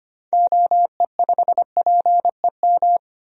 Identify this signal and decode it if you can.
Morse code, heard as a keyed tone, about 25 wpm, 700 Hz. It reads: OE5PEM
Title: DXCC-100 Call Signs - 6 Characters - 25wpm